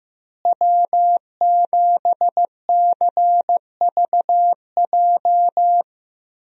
Morse code, heard as a keyed tone, about 15 wpm, 700 Hz. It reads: W7CVJ
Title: USA Call Signs - 5 Characters - Repeated 2x - 15wpm